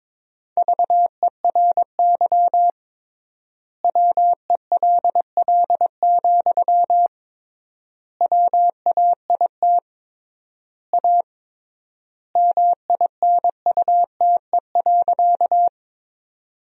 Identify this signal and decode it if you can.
Morse code, heard as a keyed tone, about 21 wpm, 700 Hz. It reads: VERY WELL, WAIT A MINUTE.
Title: 3x Word-Spacing - Sentences from Top 500 Words - 22wpm